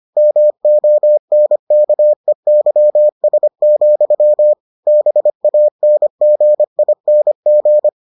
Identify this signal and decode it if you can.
Morse code, heard as a keyed tone, about 25 wpm, 600 Hz. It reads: MONKEYS, BANGING